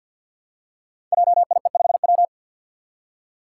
Morse code, heard as a keyed tone, about 51 wpm, 700 Hz. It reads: JIE5W
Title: Code Groups - 5 Letter:Numbers - 50wpm